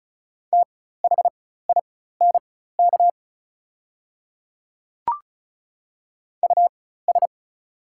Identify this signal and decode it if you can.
Morse code, heard as a keyed tone, about 36 wpm, spaced as for 16 wpm, 700 Hz. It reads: THINK US
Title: 4x Character Spacing - Top 200 Words - 35wpm